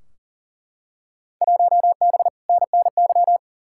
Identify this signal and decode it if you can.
Morse code, heard as a keyed tone, about 40 wpm, 700 Hz. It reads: 1B NNY
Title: Field Day Exchanges - 40wpm